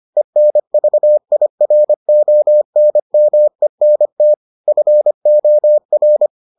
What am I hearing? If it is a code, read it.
Morse code, heard as a keyed tone, about 25 wpm, 600 Hz. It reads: ENVIRONMENT FOR